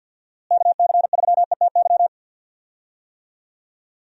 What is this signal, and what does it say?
Morse code, heard as a keyed tone, about 49 wpm, 700 Hz. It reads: KC3ETY
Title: USA Call Signs - 6 Characters - Repeated 2x - 50wpm